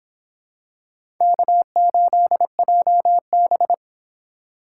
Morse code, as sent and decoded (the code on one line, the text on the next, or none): -.- ---.. .--- -...
K8JB